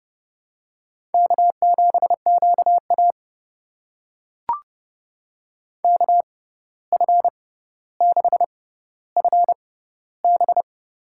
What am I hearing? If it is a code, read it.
Morse code, heard as a keyed tone, about 30 wpm, spaced as for 14 wpm, 700 Hz. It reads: K7QA KF6FB
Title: USA Call Signs - 6x Character Spacing - 30wpm